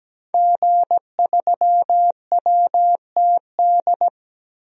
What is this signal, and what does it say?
Morse code, heard as a keyed tone, about 17 wpm, 700 Hz. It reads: G3WTD
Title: DXCC-100 Call Signs - 5 Characters - 17wpm